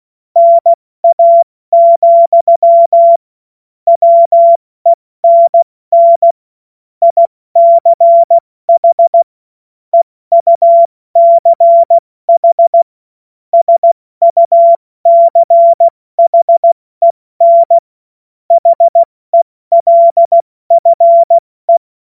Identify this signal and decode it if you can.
Morse code, heard as a keyed tone, about 16 wpm, spaced as for 14 wpm, 680 Hz. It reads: NA, WENN ICH EUCH SUCHEN HELFE
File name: Morsekurs / Witze / Witz 20. Januar 2026 70 F80